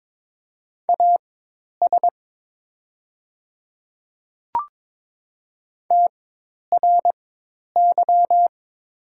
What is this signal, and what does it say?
Morse code, heard as a keyed tone, about 22 wpm, spaced as for 10 wpm, 690 Hz. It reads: AS TRY
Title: 4x Character Spacing - Top 200 Words - 22wpm